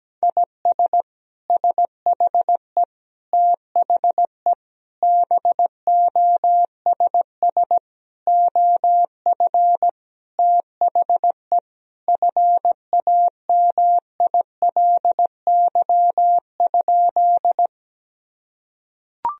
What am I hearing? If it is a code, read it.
Morse code, heard as a keyed tone, about 17 wpm, 700 Hz. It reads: IS SHE THE BOSS OF THE FAMILY?
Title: Sentences from Top 1000 Words - Review Part 4 - 17wpm